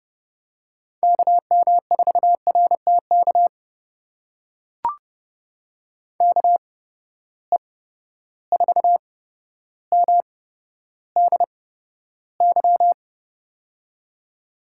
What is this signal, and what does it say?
Morse code, heard as a keyed tone, about 30 wpm, spaced as for 13 wpm, 700 Hz. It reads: KM4RTK KE4MDY
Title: USA Call Signs - 6 Characters - 8x Character Spacing - 30wpm